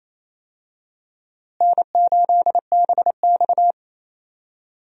Morse code, heard as a keyed tone, about 28 wpm, 700 Hz. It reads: N8BX
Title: USA Call Signs - 4 Characters - Repeated 3x - 28wpm